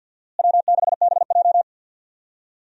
Morse code, heard as a keyed tone, about 48 wpm, 700 Hz. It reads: W6BJ
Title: USA Call Signs - 4 Characters - Repeated 2x - 50wpm